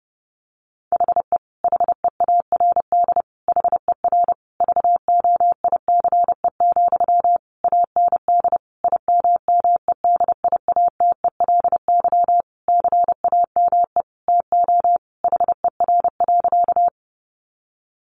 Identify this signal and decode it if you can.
Morse code, heard as a keyed tone, about 30 wpm, 700 Hz. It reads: HE HEARD HER VOICE, AND IMMEDIATELY CAME TO HER.
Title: Sentences from Top 1100 Words 30wpm